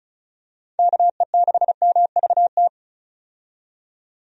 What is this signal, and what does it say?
Morse code, heard as a keyed tone, about 34 wpm, 700 Hz. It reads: KE6MVT
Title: USA Call Signs - 6 Characters - 35wpm